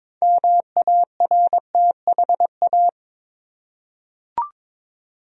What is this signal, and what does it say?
Morse code, heard as a keyed tone, about 22 wpm, 700 Hz. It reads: MARTHA
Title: USA Common Names - 22wpm